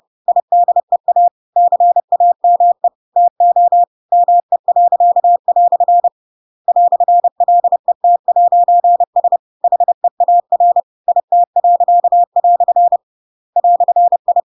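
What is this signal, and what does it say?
Morse code, heard as a keyed tone, about 30 wpm, 700 Hz. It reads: IDEA CAME TO ME." "LET'S HEAR IT." "I
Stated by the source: Chapter 2 of The Adventures of Pinocchio read as a book in Morse code (CW) at 30 wpm, by ik1hge.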